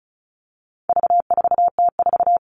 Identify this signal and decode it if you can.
Morse code, heard as a keyed tone, about 34 wpm, 700 Hz. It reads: V4T4
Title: All Cut Numbers - 4-Digit 35wpm